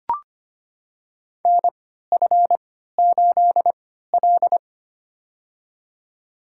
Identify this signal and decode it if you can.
Morse code, heard as a keyed tone, about 25 wpm, spaced as for 14 wpm, 700 Hz. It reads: NF8L
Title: USA Call Signs - 3x Character Spacing - 25wpm